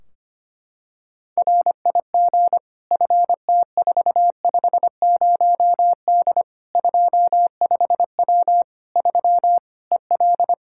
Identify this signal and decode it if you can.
Morse code, heard as a keyed tone, about 25 wpm, 700 Hz. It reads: RIG FT450D 25W 3 EL
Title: Practice QSO - Exchange 2 - Info - 25wpm